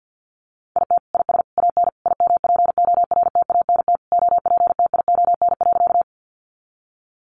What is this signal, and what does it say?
Morse code, heard as a keyed tone, about 50 wpm, 700 Hz. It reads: IT IS AN IMPORTANT OPTION.
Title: Sentences from Top 1300 Words 50wpm